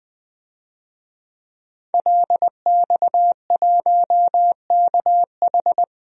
Morse code, heard as a keyed tone, about 20 wpm, 700 Hz. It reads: LX1KH
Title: DXCC-100 Call Signs - 5 Characters - Repeated 3x - 20wpm